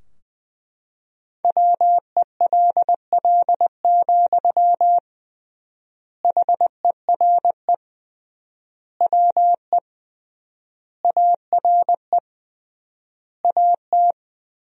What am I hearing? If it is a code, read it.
Morse code, heard as a keyed tone, about 20 wpm, 700 Hz. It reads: WELL, HERE WE ARE AT
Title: 3x Word-Spacing - Sentences from Top 300 Words - 20wpm